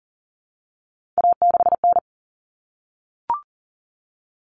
Morse code, heard as a keyed tone, about 40 wpm, 700 Hz. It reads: A6N
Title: All Cut Numbers - 3-Digit 40wpm